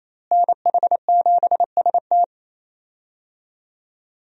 Morse code, heard as a keyed tone, about 28 wpm, 700 Hz. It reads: NH7ST
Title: USA Call Signs - 5 Characters - Repeated 3x - 28wpm